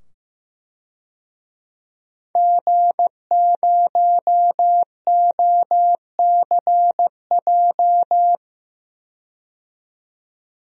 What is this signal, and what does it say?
Morse code, heard as a keyed tone, about 15 wpm, 700 Hz. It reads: G0OCJ
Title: DXCC-100 Call Signs - 5 Characters - 15wpm